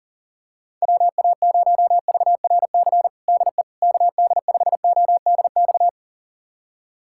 Morse code, heard as a keyed tone, about 40 wpm, 700 Hz. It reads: WA0VRC DE KD5ODX
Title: USA Callsign DE Callsign - 6 Characters - 40wpm